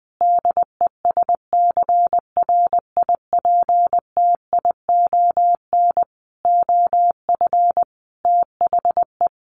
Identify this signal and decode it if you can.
Morse code, heard as a keyed tone, about 20 wpm, 700 Hz. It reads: DESCRIPTION OF THE
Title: Sentences from Top 1300 Words 20wpm